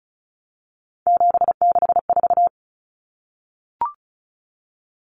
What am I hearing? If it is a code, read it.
Morse code, heard as a keyed tone, about 34 wpm, 700 Hz. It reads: 764
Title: Numbers - 3-Digit 35wpm